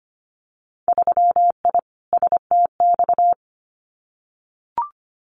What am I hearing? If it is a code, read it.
Morse code, heard as a keyed tone, about 25 wpm, 700 Hz. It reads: 3I STX
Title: Winter Field Day Exchanges 25wpm